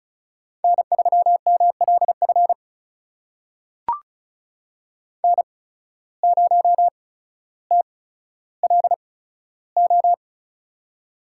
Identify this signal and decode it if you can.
Morse code, heard as a keyed tone, about 34 wpm, spaced as for 15 wpm, 700 Hz. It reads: N3MLF N0TLO
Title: USA Call Signs - 5 Characters - 8x Character Spacing - 35wpm